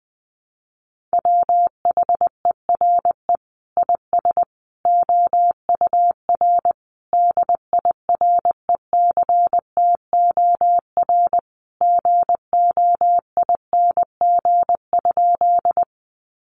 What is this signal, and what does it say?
Morse code, heard as a keyed tone, about 20 wpm, 700 Hz. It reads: WHERE IS OUR DIRECTOR GOING?
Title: Sentences from Top 1300 Words 20wpm